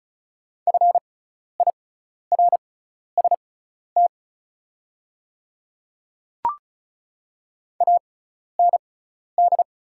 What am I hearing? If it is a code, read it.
Morse code, heard as a keyed tone, about 36 wpm, spaced as for 12 wpm, 700 Hz. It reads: FIRST AND
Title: 6x Character Spacing - Top 100 Words - 35wpm